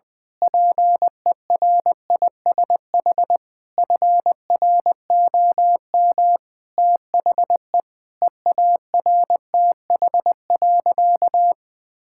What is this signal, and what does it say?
Morse code, heard as a keyed tone, about 20 wpm, 700 Hz. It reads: PERISH FROM THE EARTH.
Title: Sentences with 9-Word Phrases - 20wpm